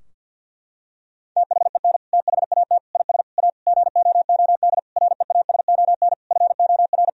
Morse code, heard as a keyed tone, about 50 wpm, 700 Hz. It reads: THEN THAT IS A GOOD REASON FOR
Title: Sentences from Top 1000 Words - Review Part 4 - 50wpm